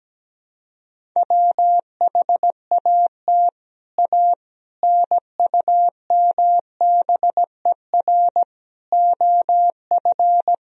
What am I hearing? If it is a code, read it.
Morse code, heard as a keyed tone, about 17 wpm, 700 Hz. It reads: WHAT A NUMBER OF
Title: Sentences from Top 1000 Words - Review Part 1 - 17wpm